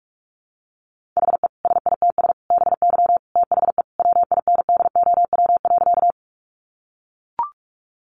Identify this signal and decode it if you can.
Morse code, heard as a keyed tone, about 44 wpm, 700 Hz. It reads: HE SITS BY THE WINDOW.
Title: Additional Easy Phrases 45wpm